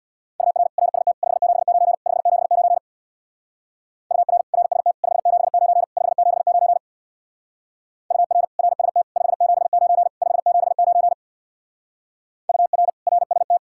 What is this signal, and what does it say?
Morse code, heard as a keyed tone, about 63 wpm, 700 Hz. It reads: UR RST 579 579 UR RST 579 579 UR RST 579 579 UR RST